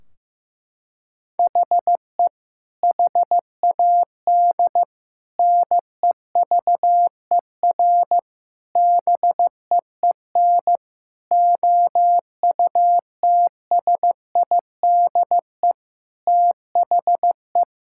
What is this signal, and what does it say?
Morse code, heard as a keyed tone, about 15 wpm, 700 Hz. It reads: HE HAD NEVER BEEN OUTSIDE THE
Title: Sentences from Top 1600 Words 15wpm